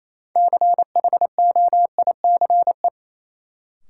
Morse code, heard as a keyed tone, about 28 wpm, 700 Hz. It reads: CHOICE
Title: Sentences from Top 900 Words - New Word Drill - 28wpm